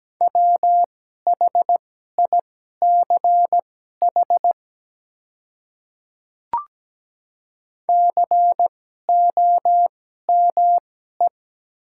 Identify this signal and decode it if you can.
Morse code, heard as a keyed tone, about 17 wpm, spaced as for 12 wpm, 700 Hz. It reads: WHICH COME
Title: 2x Character Spacing - Top 100 Words - 17wpm